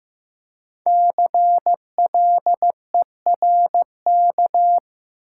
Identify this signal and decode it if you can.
Morse code, heard as a keyed tone, about 15 wpm, 700 Hz. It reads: CLERK
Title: Five Letter Words - 15wpm